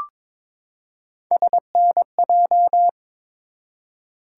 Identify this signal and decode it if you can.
Morse code, heard as a keyed tone, about 22 wpm, 700 Hz. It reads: SNJ
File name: ARRL Sections v2 - 22wpm